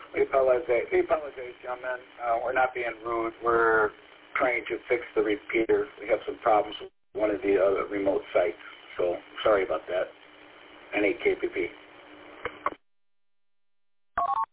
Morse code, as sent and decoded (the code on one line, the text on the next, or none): none